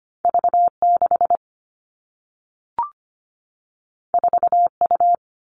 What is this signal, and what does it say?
Morse code, heard as a keyed tone, about 25 wpm, 700 Hz. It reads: V6 4U
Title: All Cut Numbers - 2-Digit 25wpm